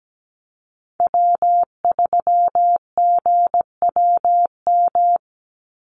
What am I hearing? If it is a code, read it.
Morse code, heard as a keyed tone, about 17 wpm, 700 Hz. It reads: W3GWM
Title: Call Signs Repeated 2x - 17wpm